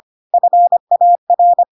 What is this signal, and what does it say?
Morse code, heard as a keyed tone, about 25 wpm, 700 Hz. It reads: FAR
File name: Pinocchio / Pinocchio ch6 25 25 ik1hge